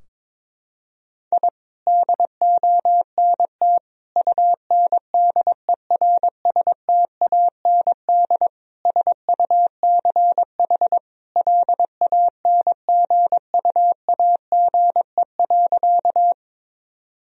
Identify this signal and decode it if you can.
Morse code, heard as a keyed tone, about 22 wpm, 700 Hz. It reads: I DONT UNDERSTAND SUCH LANGUAGE.
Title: Sentences from Top 1000 Words - Review Part 3 - 22wpm